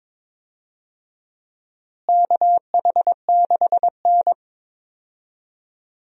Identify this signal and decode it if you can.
Morse code, heard as a keyed tone, about 22 wpm, 700 Hz. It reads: KH6N
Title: DXCC-100 Call Signs - 4 Characters - Repeated 3x - 22wpm